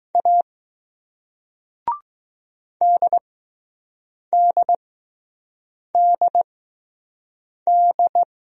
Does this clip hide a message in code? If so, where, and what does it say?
Yes: Morse code from 0.2 to 8.2 s, heard as a keyed tone, about 20 wpm, spaced as for 6 wpm, 700 Hz. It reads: A DDDD